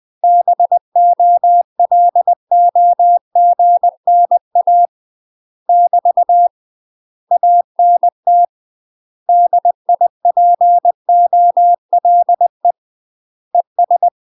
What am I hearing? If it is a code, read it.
Morse code, heard as a keyed tone, about 20 wpm, 700 Hz. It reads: BOLOGNA <BT> ANT DIPOLE ES
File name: UK MASTER QSOs TRAINING / 20WPM / UK MASTER QSOs 0068